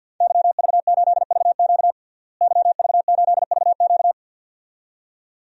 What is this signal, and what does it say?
Morse code, heard as a keyed tone, about 50 wpm, 700 Hz. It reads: YV8VQ YV8VQ
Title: DXCC-100 Call Signs - 5 Characters - Repeated 2x - 50wpm